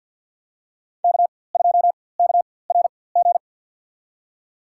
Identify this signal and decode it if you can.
Morse code, heard as a keyed tone, about 51 wpm, spaced as for 23 wpm, 700 Hz. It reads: K2KRG
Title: USA Call Signs - 5 Characters - 4x Character Spacing - 50wpm